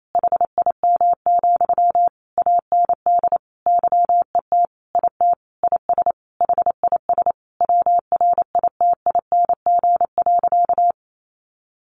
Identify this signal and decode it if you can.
Morse code, heard as a keyed tone, about 28 wpm, 700 Hz. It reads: HIM, AND YET IT IS HIS WRITING.
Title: Sentences from Top 800 Words 28wpm